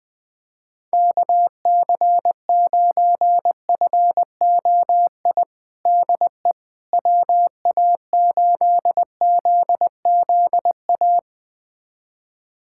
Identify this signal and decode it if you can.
Morse code, heard as a keyed tone, about 20 wpm, 700 Hz. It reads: KC9FOI DE WA8ZZA
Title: USA Callsign DE Callsign - 6 Characters - 20wpm